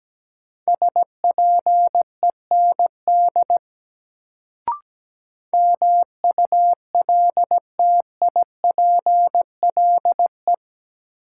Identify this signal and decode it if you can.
Morse code, heard as a keyed tone, about 17 wpm, 700 Hz. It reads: SPEND MULTIPLE
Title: Sets of 5 Words - New Word Drill - 17wpm